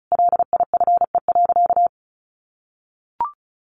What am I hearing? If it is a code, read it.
Morse code, heard as a keyed tone, about 34 wpm, 700 Hz. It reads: LIFE.
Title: Sentences from Top 1500 Words 35wpm